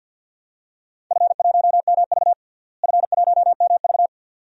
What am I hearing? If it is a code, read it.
Morse code, heard as a keyed tone, about 52 wpm, 700 Hz. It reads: F1MV F1MV
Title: DXCC-100 Call Signs - 4 Characters - Repeated 2x - 50wpm